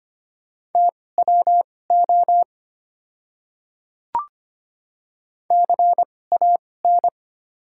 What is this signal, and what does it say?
Morse code, heard as a keyed tone, about 25 wpm, spaced as for 18 wpm, 690 Hz. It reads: TWO CAN
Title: 2x Character Spacing - Top 200 Words - 25wpm